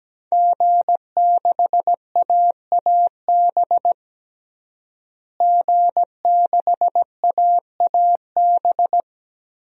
Morse code, heard as a keyed tone, about 17 wpm, 700 Hz. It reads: G6AAB G6AAB
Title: DXCC-100 Call Signs - 5 Characters - Repeated 2x - 17wpm